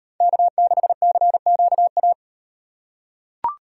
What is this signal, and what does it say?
Morse code, heard as a keyed tone, about 36 wpm, 700 Hz. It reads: K6CQA